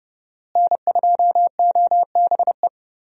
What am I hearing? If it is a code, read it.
Morse code, heard as a keyed tone, about 31 wpm, 700 Hz. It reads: N2OBE
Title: USA Call Signs - 5 Characters - Repeated 3x - 30wpm